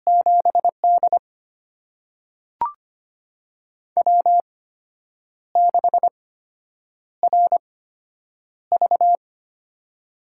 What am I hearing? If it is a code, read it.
Morse code, heard as a keyed tone, about 25 wpm, spaced as for 8 wpm, 700 Hz. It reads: 7D W6RV
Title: USA Call Signs - 4 Characters - 8x Character Spacing - 25wpm